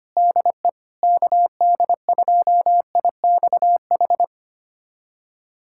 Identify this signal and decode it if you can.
Morse code, heard as a keyed tone, about 25 wpm, 700 Hz. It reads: DE KD2IXH
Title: USA Callsign DE Callsign - 6 Characters - 25wpm